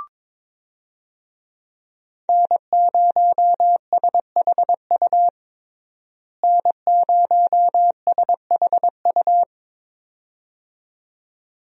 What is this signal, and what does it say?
Morse code, heard as a keyed tone, about 22 wpm, 700 Hz. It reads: N0SHU N0SHU
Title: USA Call Signs - 5 Characters - Repeated 2x - 22wpm